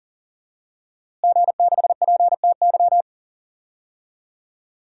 Morse code, heard as a keyed tone, about 40 wpm, 700 Hz. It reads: G6PTY
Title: DXCC-100 Call Signs - 5 Characters - Repeated 2x - 40wpm